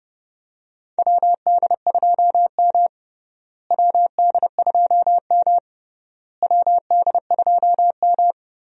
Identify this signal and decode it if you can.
Morse code, heard as a keyed tone, about 30 wpm, 700 Hz. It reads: WD2M WD2M WD2M
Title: USA Call Signs - 4 Characters - Repeated 3x - 30wpm